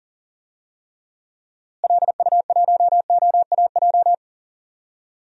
Morse code, heard as a keyed tone, about 40 wpm, 700 Hz. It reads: LU1OAJ
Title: DXCC-100 Call Signs - 6 Characters - Repeated 2x - 40wpm